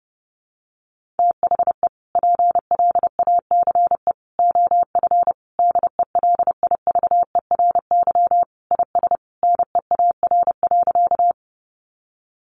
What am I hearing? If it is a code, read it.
Morse code, heard as a keyed tone, about 30 wpm, 700 Hz. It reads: THE PLACE OF DELIVERY IS NEAR.
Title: Sentences from Top 1500 Words 30wpm